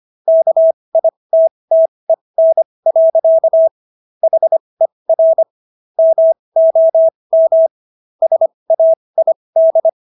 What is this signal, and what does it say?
Morse code, heard as a keyed tone, about 25 wpm, spaced as for 20 wpm, 650 Hz. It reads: KITTEN. HER MOM SAID